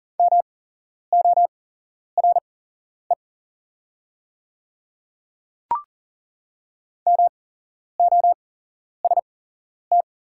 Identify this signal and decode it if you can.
Morse code, heard as a keyed tone, about 39 wpm, spaced as for 11 wpm, 700 Hz. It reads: MORE MOST
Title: 8x Character Spacing - Top 200 Words - 40wpm